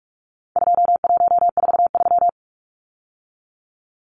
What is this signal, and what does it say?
Morse code, heard as a keyed tone, about 44 wpm, 700 Hz. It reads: 2143
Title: Numbers - 4-Digit 45wpm